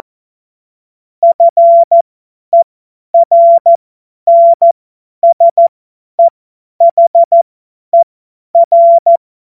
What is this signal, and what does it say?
Morse code, heard as a keyed tone, about 14 wpm, spaced as for 10 wpm, 680 Hz. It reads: FERNSEHER
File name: Morsekurs / Witze / Witz 27. September 2022 50 F70